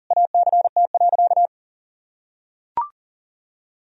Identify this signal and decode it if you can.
Morse code, heard as a keyed tone, about 37 wpm, 700 Hz. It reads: ACT.
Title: Sentences from Top 1000 Words - Review Part 1 - 40wpm